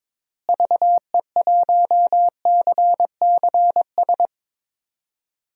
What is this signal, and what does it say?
Morse code, heard as a keyed tone, about 22 wpm, 700 Hz. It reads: VE1CCS
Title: DXCC-100 Call Signs - 6 Characters - Repeated 2x - 22wpm